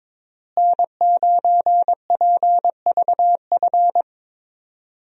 Morse code, heard as a keyed tone, about 22 wpm, 700 Hz. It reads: N9PVF
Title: USA Call Signs - 5 Characters - Repeated 2x - 22wpm